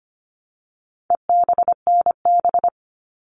Morse code, heard as a keyed tone, about 25 wpm, 700 Hz. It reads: EBNB
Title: All Cut Numbers - 4-Digit 25wpm